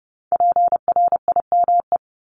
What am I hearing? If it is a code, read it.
Morse code, heard as a keyed tone, about 30 wpm, 690 Hz. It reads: PRIME
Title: Multi-letter Phonemes Repeated 3x 30wpm